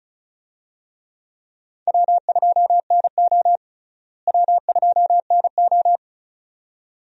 Morse code, heard as a keyed tone, about 34 wpm, 700 Hz. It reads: W2NO W2NO
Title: USA Call Signs - 4 Characters - Repeated 2x - 35wpm